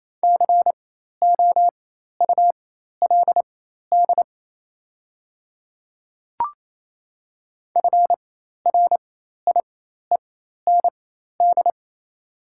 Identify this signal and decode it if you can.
Morse code, heard as a keyed tone, about 28 wpm, spaced as for 13 wpm, 700 Hz. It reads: COULD FRIEND
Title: 4x Character Spacing - Top 500 Words - 28wpm